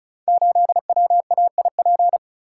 Morse code, heard as a keyed tone, about 34 wpm, 700 Hz. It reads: 8WAIP
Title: Code Groups - 5 Letter:Numbers - 35wpm